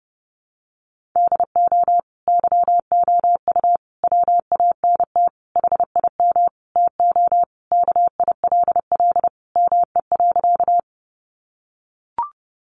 Morse code, heard as a keyed tone, about 30 wpm, 700 Hz. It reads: DO YOU WANT HIM TO KILL ME.
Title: Sentences from Top 1400 Words 30wpm